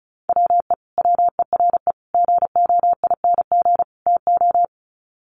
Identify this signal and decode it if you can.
Morse code, heard as a keyed tone, about 34 wpm, 700 Hz. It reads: WE WERE GOING TO
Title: Sets of 4 Words 35wpm